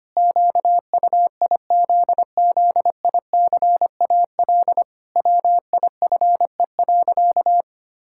Morse code, heard as a keyed tone, about 25 wpm, 700 Hz. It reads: QUIZZICAL WIFE.
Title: Pangram Challenge v2 - 25wpm